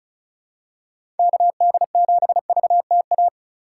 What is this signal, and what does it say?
Morse code, heard as a keyed tone, about 36 wpm, 700 Hz. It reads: KD7VTA
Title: USA Call Signs - 6 Characters - 3x Character Spacing - 35wpm